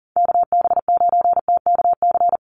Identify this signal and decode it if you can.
Morse code, heard as a keyed tone, about 40 wpm, 700 Hz. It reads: KB9TKC
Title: USA Call Signs Repeated 2x 40wpm